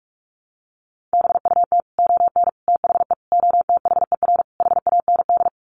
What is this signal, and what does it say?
Morse code, heard as a keyed tone, about 44 wpm, 700 Hz. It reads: BUT ON THE OTHER HAND
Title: Sets of 5 Words 45wpm